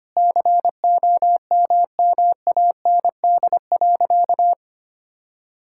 Morse code, heard as a keyed tone, about 25 wpm, 700 Hz. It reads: COMMAND.
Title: Sentences from Top 1600 Words 25wpm